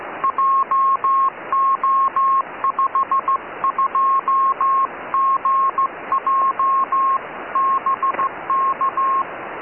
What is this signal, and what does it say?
Morse code, heard as a keyed tone, about 15 wpm, 1080 Hz, with radio noise. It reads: JO52GJ BK
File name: dj3ak 2m